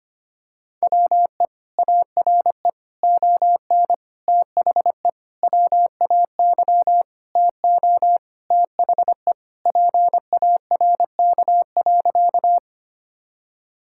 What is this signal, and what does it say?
Morse code, heard as a keyed tone, about 25 wpm, 700 Hz. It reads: WE ARE ON THE WAY TO THE PARK.
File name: practice / 25wpm / Sentences with 4-Word Phrases - 25wpm